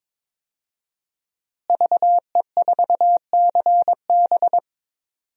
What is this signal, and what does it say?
Morse code, heard as a keyed tone, about 21 wpm, 700 Hz. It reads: VE4CB
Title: DXCC-100 Call Signs - 5 Characters - Repeated 3x - 22wpm